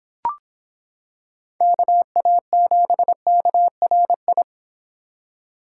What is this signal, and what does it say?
Morse code, heard as a keyed tone, about 26 wpm, 700 Hz. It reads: KA7KRI